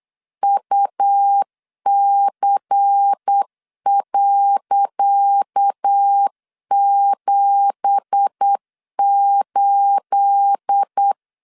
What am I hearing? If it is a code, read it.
Morse code, heard as a keyed tone, about 8 wpm, 800 Hz. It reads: UC.78
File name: LEZIONI / 10 SIGNS / 10C DOT 8wpm 3649